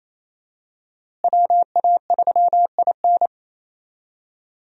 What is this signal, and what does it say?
Morse code, heard as a keyed tone, about 28 wpm, 700 Hz. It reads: WA3IN
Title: USA Call Signs - 5 Characters - Repeated 2x - 28wpm